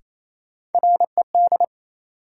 Morse code, heard as a keyed tone, about 28 wpm, 700 Hz. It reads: RED
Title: Sentences from Top 700 Words - New Word Drill - 28wpm